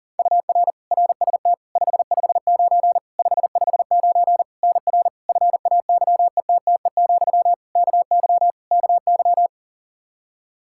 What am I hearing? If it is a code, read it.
Morse code, heard as a keyed tone, about 40 wpm, 700 Hz. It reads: UR RST 559 559 NR FAYETTE, KY KY
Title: Practice QSO - Exchange 1 - Beginning - 40wpm